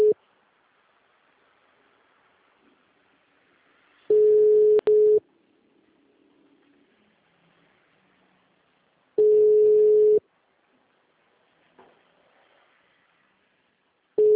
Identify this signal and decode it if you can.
Ringback tone (Europe, 425 Hz).
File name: out- -2011- - - .407